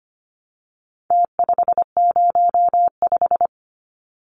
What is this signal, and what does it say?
Morse code, heard as a keyed tone, about 25 wpm, 700 Hz. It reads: T505
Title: Numbers - Contest Style - 4-Digit 25wpm